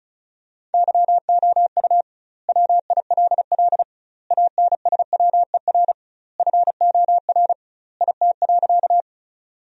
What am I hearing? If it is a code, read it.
Morse code, heard as a keyed tone, about 36 wpm, 700 Hz. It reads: YOU WILL ANSWER FOR IT.
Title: 2x Word-Spacing - Sentences from Top 300 Words - 35wpm